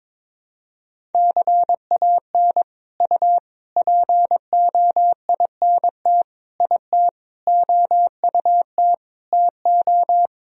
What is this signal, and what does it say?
Morse code, heard as a keyed tone, about 21 wpm, 700 Hz. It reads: CAN U POINT IT OUT TO
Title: Sentences from Top 200 Words and Abbreviations 22wpm